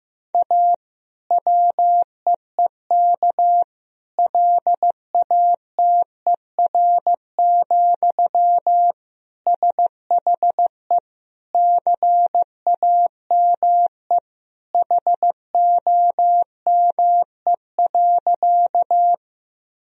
Morse code, heard as a keyed tone, about 15 wpm, 700 Hz. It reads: A WEEK LATER, SHE CAME HOME.
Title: Sentences from Top 1000 Words - Review Part 4 - 15wpm